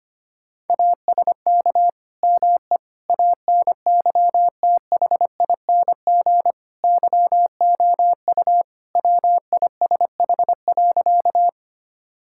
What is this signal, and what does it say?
Morse code, heard as a keyed tone, about 25 wpm, 700 Hz. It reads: ASK ME ANYTHING YOU WISH.
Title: Sentences from Top 1600 Words 25wpm